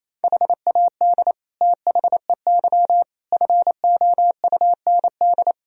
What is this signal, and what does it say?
Morse code, heard as a keyed tone, about 28 wpm, 700 Hz. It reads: HAD THEY FOUND
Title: Sentences from Top 1000 Words - Review Part 2 - 28wpm